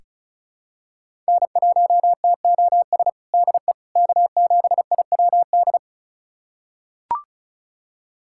Morse code, heard as a keyed tone, about 36 wpm, 700 Hz. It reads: N1TOS DE K7IWD
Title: USA Callsign DE Callsign - 5 Characters - 35wpm